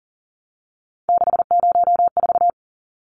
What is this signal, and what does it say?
Morse code, heard as a keyed tone, about 40 wpm, 700 Hz. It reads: 604
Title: Numbers - 3-Digit 40wpm